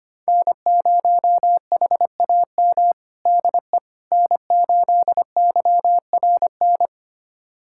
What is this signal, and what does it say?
Morse code, heard as a keyed tone, about 25 wpm, 700 Hz. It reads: N0HAM DE N8YRN
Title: USA Callsign DE Callsign - 5 Characters - 25wpm